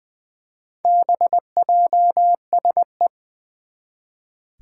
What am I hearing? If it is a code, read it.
Morse code, heard as a keyed tone, about 21 wpm, 700 Hz. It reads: BJSE
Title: Four Challenging Letter-Number Combinations 20wpm